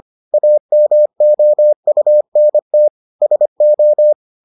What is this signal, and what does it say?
Morse code, heard as a keyed tone, about 25 wpm, 600 Hz. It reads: AMOUNT SO